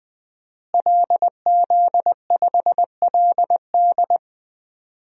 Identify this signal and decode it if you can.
Morse code, heard as a keyed tone, about 20 wpm, 700 Hz. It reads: LZ5LD
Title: DXCC-100 Call Signs - 5 Characters - Repeated 2x - 20wpm